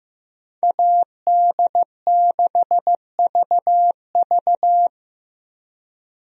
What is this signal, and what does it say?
Morse code, heard as a keyed tone, about 15 wpm, 700 Hz. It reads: AD6VV
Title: USA Call Signs - 5 Characters - 15wpm